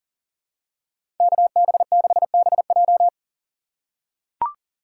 Morse code, heard as a keyed tone, about 40 wpm, 700 Hz. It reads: KB6BJ
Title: USA Call Signs - 4x Character Spacing - 40wpm